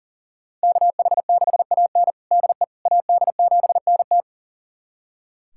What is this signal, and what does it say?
Morse code, heard as a keyed tone, about 40 wpm, 700 Hz. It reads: KH6AN DE AD7NT
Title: USA Callsign DE Callsign 40wpm